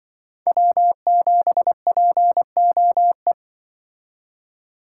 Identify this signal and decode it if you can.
Morse code, heard as a keyed tone, about 24 wpm, 700 Hz. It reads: W7POE
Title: Speed Racing - USA Call Signs 20wpm